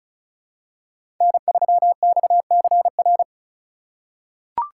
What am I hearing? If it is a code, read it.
Morse code, heard as a keyed tone, about 36 wpm, 700 Hz. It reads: N3XCR
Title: USA Call Signs - 5 Characters - 4x Character Spacing - 35wpm